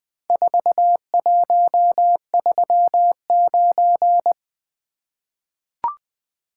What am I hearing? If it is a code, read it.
Morse code, heard as a keyed tone, about 20 wpm, 700 Hz. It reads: 4139
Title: Four Letter-Number Combinations - 20wpm